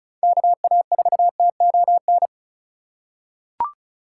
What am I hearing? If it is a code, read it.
Morse code, heard as a keyed tone, about 34 wpm, 700 Hz. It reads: KA4TON
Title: USA Call Signs - 6 Characters - 2x Character Spacing - 35wpm